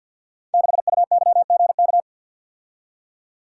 Morse code, heard as a keyed tone, about 52 wpm, 700 Hz. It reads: 6UYGK
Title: Code Groups - 5 Letter:Numbers - 50wpm